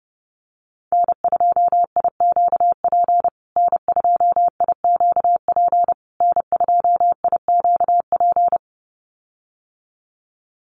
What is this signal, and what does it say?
Morse code, heard as a keyed tone, about 30 wpm, 700 Hz. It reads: N2IQP N2IQP N2IQP
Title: USA Call Signs Repeated 3x 30wpm